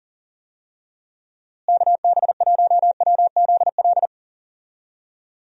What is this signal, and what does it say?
Morse code, heard as a keyed tone, about 40 wpm, 700 Hz. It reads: KB1WZL
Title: USA Call Signs - 6 Characters - Repeated 2x - 40wpm